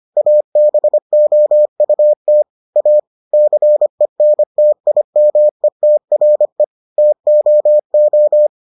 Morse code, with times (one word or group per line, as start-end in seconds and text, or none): 0.2-2.4 ABOUT
2.8-3.0 A
3.3-6.6 CENTIMETRE
7.0-8.5 TOO